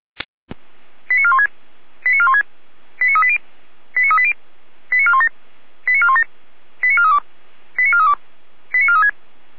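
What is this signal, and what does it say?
Five-tone selective call (ZVEI), heard as quick runs of five tones, tones of about 70 ms. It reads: 89416 89416 89380 89380 89417 89417 89432 89432 89436
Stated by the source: Five-tone ZVEI selective call received by SDR, fire brigade Orlamünde (feuerwehr-orlamuende).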